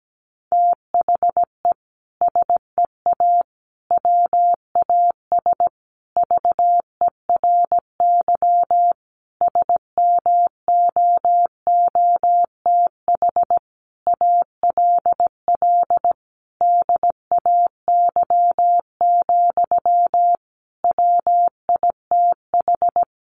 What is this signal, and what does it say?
Morse code, heard as a keyed tone, about 17 wpm, 700 Hz. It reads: THE SEA WAS VERY SMOOTH ALL DAY, WITH
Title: Sentences from Top 1400 Words 17wpm